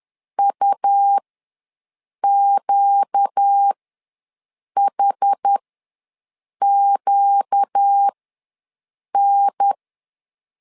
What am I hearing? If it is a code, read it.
Morse code, heard as a keyed tone, about 10 wpm, spaced as for 6 wpm, 800 Hz. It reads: UQHQN
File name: LEZIONI / 4 Q ws 10wpm 2306